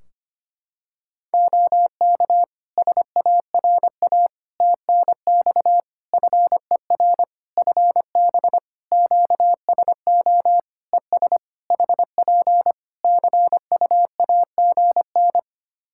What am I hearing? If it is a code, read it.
Morse code, heard as a keyed tone, about 25 wpm, 700 Hz. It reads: OK SARA TNX FER FB QSO ES HP CUAGN
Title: Practice QSO - Exchange 4 - Ending - 25wpm